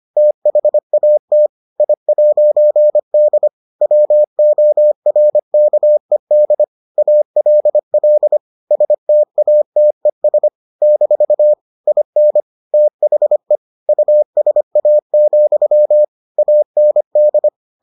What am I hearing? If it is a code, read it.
Morse code, heard as a keyed tone, about 25 wpm, 600 Hz. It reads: THAT I'D WORKED ALL STATES - IN THE USA, AND